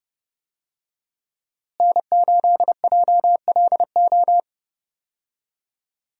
Morse code, heard as a keyed tone, about 30 wpm, 700 Hz. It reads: N8JLO
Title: USA Call Signs - 5 Characters - Repeated 3x - 30wpm